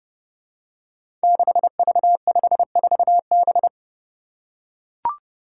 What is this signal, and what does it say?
Morse code, heard as a keyed tone, about 28 wpm, 700 Hz. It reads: 6V54B
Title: Five Challenging Letter-Number Combinations 30wpm